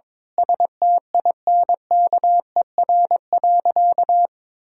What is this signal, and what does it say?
Morse code, heard as a keyed tone, about 21 wpm, 700 Hz. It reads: STINKER.
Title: Pangram Challenge v2 - 22wpm